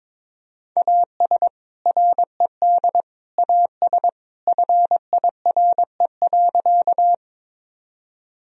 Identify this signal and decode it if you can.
Morse code, heard as a keyed tone, about 22 wpm, 700 Hz. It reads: AS RED AS FIRE.
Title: Sentences from Top 1000 Words - Review Part 2 - 22wpm